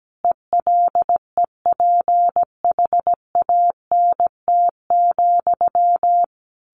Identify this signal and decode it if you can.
Morse code, heard as a keyed tone, about 17 wpm, 700 Hz. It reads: ELEPHANT,
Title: A Sound Of Thunder 17wpm